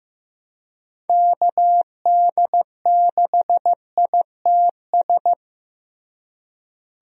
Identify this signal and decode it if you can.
Morse code, heard as a keyed tone, about 16 wpm, 700 Hz. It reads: KD6ITS
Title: USA Call Signs - 6 Characters - Repeated 2x - 15wpm